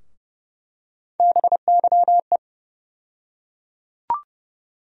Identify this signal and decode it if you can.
Morse code, heard as a keyed tone, about 29 wpm, 700 Hz. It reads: BYE
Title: USA Call Sign Postfixes - 3 Letters 30wpm